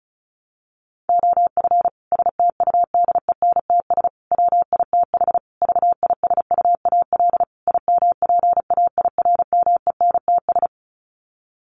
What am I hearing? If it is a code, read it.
Morse code, heard as a keyed tone, about 34 wpm, 700 Hz. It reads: OF STUDENTS WITH VISUAL IMPAIRMENTS
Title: Sets of 5 Words Encore 35wpm